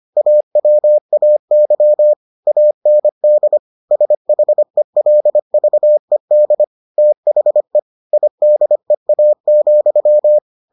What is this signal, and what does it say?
Morse code, heard as a keyed tone, about 25 wpm, 600 Hz. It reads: AWAY AND SHELVED THE IDEA,